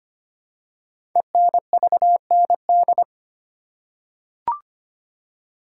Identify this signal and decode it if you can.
Morse code, heard as a keyed tone, about 25 wpm, 700 Hz. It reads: ENVND
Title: Code Groups - 5 Letters - 25wpm